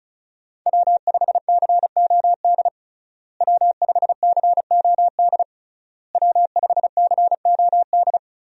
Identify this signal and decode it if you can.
Morse code, heard as a keyed tone, about 36 wpm, 700 Hz. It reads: W5COD W5COD W5COD
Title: USA Call Signs - 5 Characters - Repeated 3x - 35wpm